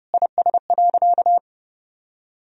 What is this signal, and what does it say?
Morse code, heard as a keyed tone, about 32 wpm, 700 Hz. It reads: IS.
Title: Sentences from Top 1000 Words - Review Part 3 - 30wpm